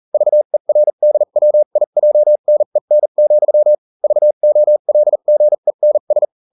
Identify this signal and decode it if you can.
Morse code, heard as a keyed tone, about 40 wpm, spaced as for 35 wpm, 600 Hz. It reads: VERDWIJNEN, VOLGENS